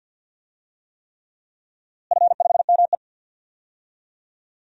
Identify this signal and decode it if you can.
Morse code, heard as a keyed tone, about 50 wpm, 700 Hz. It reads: F5ME
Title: DXCC-100 Call Signs - 4 Characters - Repeated 3x - 50wpm